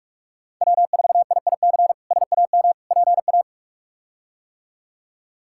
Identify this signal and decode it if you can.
Morse code, heard as a keyed tone, about 42 wpm, 700 Hz. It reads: W4IIC SAM PA
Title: K1USN SST QSO Practice - 45wpm